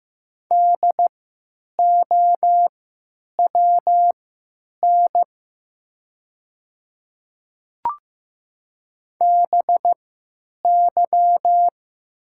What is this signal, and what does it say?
Morse code, heard as a keyed tone, about 15 wpm, spaced as for 8 wpm, 700 Hz. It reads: DOWN BY
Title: 3x Character Spacing - Top 100 Words - 15wpm